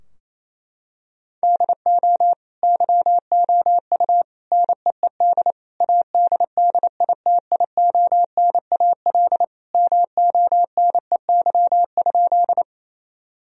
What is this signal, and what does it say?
Morse code, heard as a keyed tone, about 28 wpm, 700 Hz. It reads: DO YOU NEED ADDITIONAL MONEY?
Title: Sentences from Top 1000 Words - Review Part 4 - 28wpm